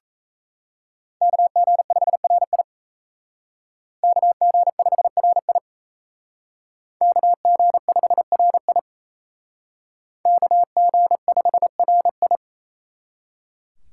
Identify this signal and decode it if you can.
Morse code, heard as a keyed tone, about 35 wpm, 700 Hz. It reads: KG5RI KG5RI KG5RI KG5RI